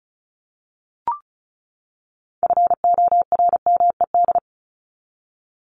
Morse code, heard as a keyed tone, about 34 wpm, 700 Hz. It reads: FORMED
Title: ED Suffix - Top 100 Words with Base Words 35wpm